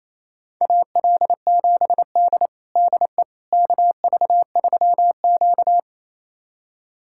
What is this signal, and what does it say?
Morse code, heard as a keyed tone, about 28 wpm, 700 Hz. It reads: AL7D DE KV3Q
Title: USA Callsign DE Callsign - 4 Characters - 28wpm